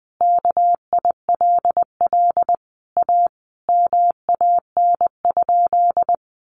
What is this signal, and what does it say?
Morse code, heard as a keyed tone, about 20 wpm, 700 Hz. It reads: KILL A MAN?
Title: Sentences from Top 1400 Words 20wpm